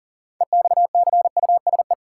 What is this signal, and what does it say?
Morse code, heard as a keyed tone, about 40 wpm, 700 Hz. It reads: EXCUSE
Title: Sentences from Top 700 Words - New Word Drill - 40wpm